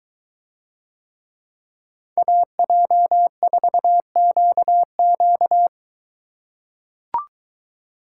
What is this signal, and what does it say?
Morse code, heard as a keyed tone, about 21 wpm, 700 Hz. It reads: AJ4QQ